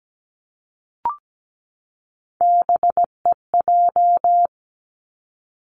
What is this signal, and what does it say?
Morse code, heard as a keyed tone, about 17 wpm, 700 Hz. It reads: BEJ
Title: Three Challenging Letter-Number Combinations 17wpm